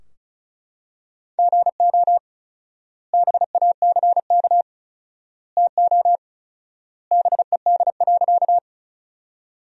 Morse code, heard as a keyed tone, about 34 wpm, 700 Hz. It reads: GO BACK TO BED.
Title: 4x Word-Spacing - Sentences from Top 500 Words - 35wpm